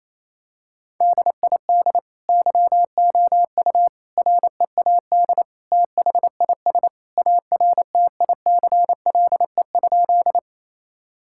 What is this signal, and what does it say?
Morse code, heard as a keyed tone, about 28 wpm, 700 Hz. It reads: DID YOU READ THIS ARTICLE?
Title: Sentences from Top 1000 Words - Review Part 5 - 28wpm